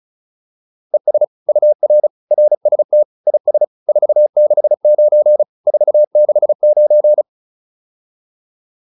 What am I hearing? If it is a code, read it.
Morse code, heard as a keyed tone, about 34 wpm, 600 Hz. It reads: ES UR RST IS 469 469